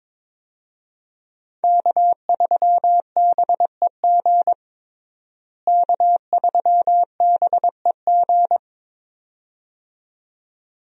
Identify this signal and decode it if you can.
Morse code, heard as a keyed tone, about 22 wpm, 700 Hz. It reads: K3BEG K3BEG
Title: USA Call Signs - 5 Characters - Repeated 2x - 22wpm